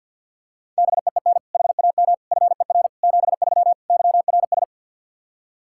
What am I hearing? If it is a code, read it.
Morse code, heard as a keyed tone, about 52 wpm, 700 Hz. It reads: BEEN HAM FER 73 YRS
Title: K1USN SST QSO Practice 50wpm